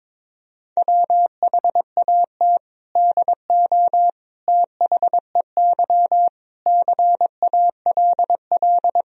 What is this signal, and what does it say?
Morse code, heard as a keyed tone, about 22 wpm, 700 Hz. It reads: WHAT DO THEY CALL
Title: Sentences from Top 100 Words and Abbreviations 22wpm